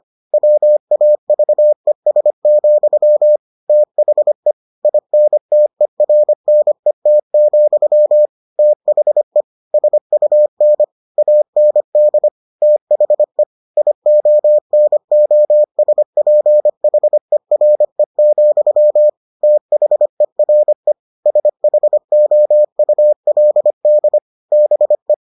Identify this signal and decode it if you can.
Morse code, heard as a keyed tone, about 25 wpm, 600 Hz. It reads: WAVES, THE INTERNET, THE SUN AND THE IONOSPHERE, THERE SHOULD BE